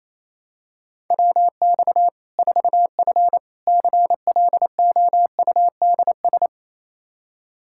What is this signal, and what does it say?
Morse code, heard as a keyed tone, about 28 wpm, 700 Hz. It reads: WX 4F CLOUDS
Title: Practice QSO - Exchange 2 - Info - 28wpm